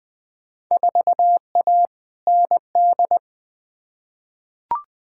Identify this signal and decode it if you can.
Morse code, heard as a keyed tone, about 20 wpm, 700 Hz. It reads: 4A ND
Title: Field Day Exchanges Encore - 20wpm